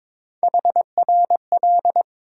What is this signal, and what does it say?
Morse code, heard as a keyed tone, about 22 wpm, 700 Hz. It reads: HRL
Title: USA Call Sign Postfixes - 3 Letters 22wpm